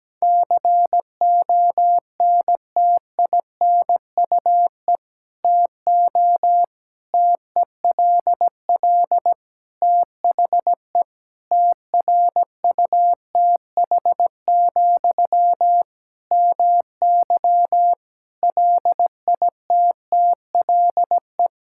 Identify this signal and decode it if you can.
Morse code, heard as a keyed tone, about 17 wpm, 700 Hz. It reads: CONTINUE TO TELL THE TRUTH, MY LITTLE
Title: Sentences from Top 1000 Words - Review Part 6 - 17wpm